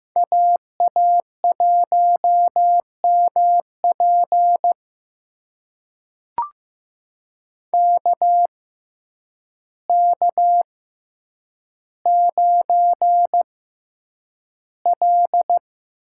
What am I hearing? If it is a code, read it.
Morse code, heard as a keyed tone, about 15 wpm, spaced as for 8 wpm, 700 Hz. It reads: AA1MP KK9L
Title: USA Call Signs - 6x Character Spacing - 15wpm